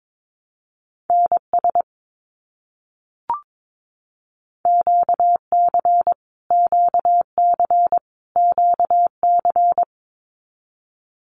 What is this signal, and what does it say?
Morse code, heard as a keyed tone, about 22 wpm, 700 Hz. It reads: NS QC QC QC
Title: Canadian Provinces and Territories - Repeated 3x - 22wpm